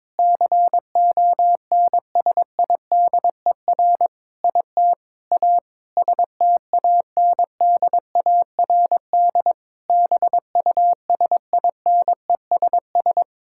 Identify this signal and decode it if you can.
Morse code, heard as a keyed tone, about 22 wpm, 700 Hz. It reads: CONSIDER IT A STANDARD BUSINESS
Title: Sentences from Top 1000 Words - Review Part 6 - 22wpm